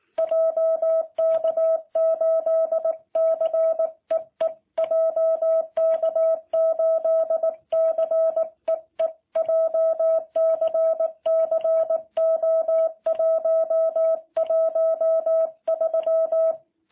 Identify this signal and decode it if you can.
Morse code, heard as a keyed tone, about 19 wpm, 660 Hz. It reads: JK8CEE JK8CEE JCC O113